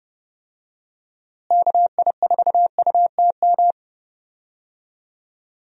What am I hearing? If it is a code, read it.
Morse code, heard as a keyed tone, about 30 wpm, 700 Hz. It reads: KI4UTM
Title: USA Call Signs - 6 Characters - Repeated 3x - 30wpm